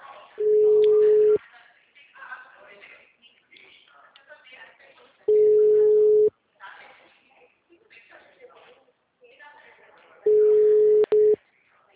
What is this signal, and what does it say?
Ringback tone (Europe, 425 Hz), with speech.
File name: out- -2056